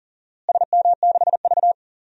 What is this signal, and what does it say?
Morse code, heard as a keyed tone, about 40 wpm, 700 Hz. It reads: SM6V
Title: DXCC-100 Call Signs - 4 Characters - Repeated 3x - 40wpm